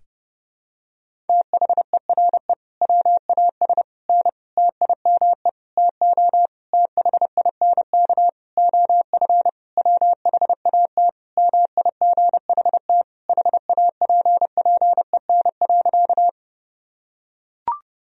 Morse code, heard as a keyed tone, about 32 wpm, 700 Hz. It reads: THERE WAS N TIME TO THINK OF WHAT MIGHT HAPPEN.
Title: Sentences from Top 400 Words and Abbreviations 30wpm